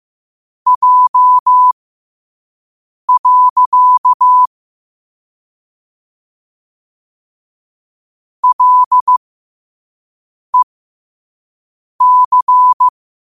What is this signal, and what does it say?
Morse code, heard as a keyed tone, about 15 wpm, spaced as for 5 wpm, 1000 Hz. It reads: J. LEC
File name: qotdfiles / 5wpm